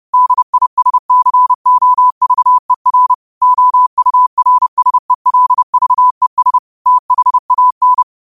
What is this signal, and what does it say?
Morse code, heard as a keyed tone, about 30 wpm, 1000 Hz. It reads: DISCOVER OURSELVES THAN